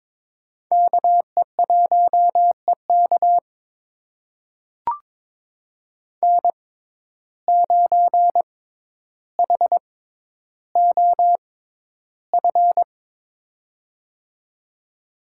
Morse code, heard as a keyed tone, about 22 wpm, spaced as for 11 wpm, 700 Hz. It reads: KE1EK N9HOF
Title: USA Call Signs - 5 Characters - 6x Character Spacing - 22wpm